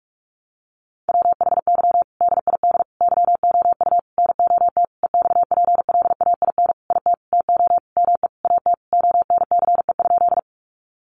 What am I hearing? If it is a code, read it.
Morse code, heard as a keyed tone, about 44 wpm, 700 Hz. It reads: WHY DID YOU NOT EXPLAIN IT TO ME AT ONCE?
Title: Sentences from Top 1100 Words 45wpm